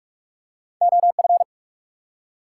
Morse code, heard as a keyed tone, about 44 wpm, 700 Hz. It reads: OF
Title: Two Letter Words - 45wpm